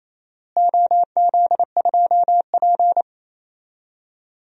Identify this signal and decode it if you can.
Morse code, heard as a keyed tone, about 28 wpm, 700 Hz. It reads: OZ2P
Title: DXCC-100 Call Signs - 4 Characters - Repeated 3x - 28wpm